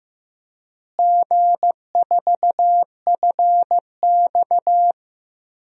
Morse code, heard as a keyed tone, about 15 wpm, 700 Hz. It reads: G4FX
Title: DXCC-100 Call Signs - 4 Characters - Repeated 3x - 15wpm